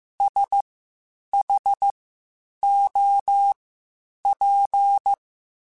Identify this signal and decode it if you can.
Morse code, heard as a keyed tone, about 15 wpm, spaced as for 8 wpm, 800 Hz. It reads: SHOP
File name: Spaced Characters / Spaced Characters-mp3s / Weather 4 24 sp @13